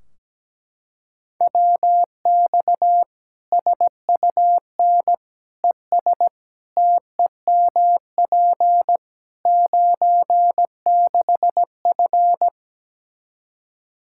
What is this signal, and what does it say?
Morse code, heard as a keyed tone, about 17 wpm, 700 Hz. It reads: WX SUN ES TEMP 96F
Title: Practice QSO - Exchange 2 - Info - 17wpm